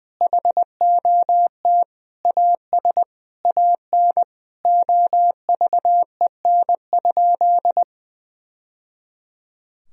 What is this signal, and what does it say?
Morse code, heard as a keyed tone, about 20 wpm, 700 Hz. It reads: HOT AS AN OVEN?
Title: Sentences from Top 1000 Words - Review Part 4 - 20wpm